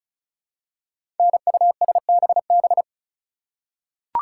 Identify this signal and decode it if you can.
Morse code, heard as a keyed tone, about 36 wpm, 700 Hz. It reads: NUSBB
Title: Code Groups - 5 Letters - 35wpm